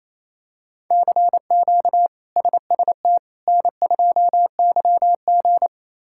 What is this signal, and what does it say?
Morse code, heard as a keyed tone, about 28 wpm, 700 Hz. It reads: CQ SST N2YG
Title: K1USN SST QSO Practice - 28wpm